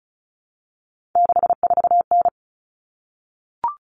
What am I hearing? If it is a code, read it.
Morse code, heard as a keyed tone, about 34 wpm, 700 Hz. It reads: 64N
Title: Common Cut Numbers - 3-Digit 35wpm